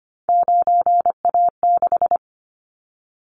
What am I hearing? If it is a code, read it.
Morse code, heard as a keyed tone, about 25 wpm, 700 Hz. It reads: 9A6
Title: Callsign Prefixes for Top 50 DXCC - 25wpm